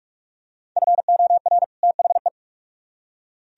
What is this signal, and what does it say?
Morse code, heard as a keyed tone, about 44 wpm, 700 Hz. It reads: FOR THE